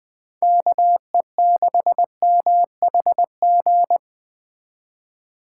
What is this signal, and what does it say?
Morse code, heard as a keyed tone, about 20 wpm, 700 Hz. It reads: KE6MHG